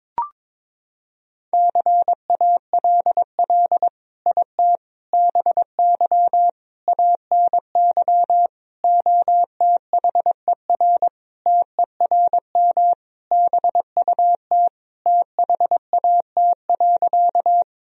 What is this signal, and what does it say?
Morse code, heard as a keyed tone, about 22 wpm, 700 Hz. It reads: CALL IT BY ANY OTHER TERM BUT THAT.
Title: Sentences from Top 1000 Words - Review Part 4 - 22wpm